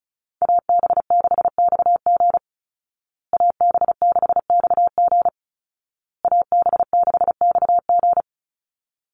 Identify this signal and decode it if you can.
Morse code, heard as a keyed tone, about 34 wpm, 700 Hz. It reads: AB6XG AB6XG AB6XG
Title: Call Signs Repeated 2x - 35wpm